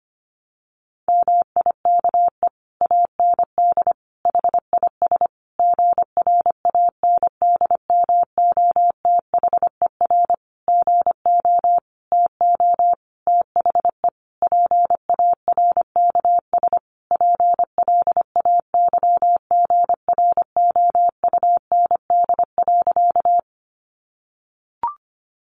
Morse code, heard as a keyed tone, about 25 wpm, 700 Hz. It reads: MIKE AND HIS GRANDMOTHER GO TO THE PARKS PLAYGROUND.
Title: Additional Easy Phrases 25wpm